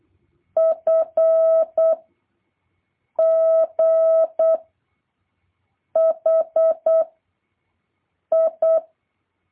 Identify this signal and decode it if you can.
Morse code, heard as a keyed tone, about 8 wpm, 650 Hz. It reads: F G H I
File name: jh8cbh / 2023cw